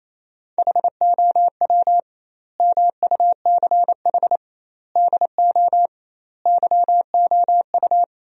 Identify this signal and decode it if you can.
Morse code, heard as a keyed tone, about 28 wpm, 700 Hz. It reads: HOW MUCH DO YOU
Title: 2x Word-Spacing - Sentences from Top 200 Words - 28wpm